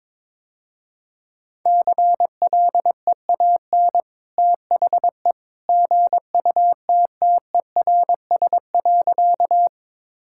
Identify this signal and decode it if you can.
Morse code, heard as a keyed tone, about 22 wpm, 700 Hz. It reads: CLEAN THE GUTTERS.
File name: practice / 22wpm / Easy Sentences with Hints - 22wpm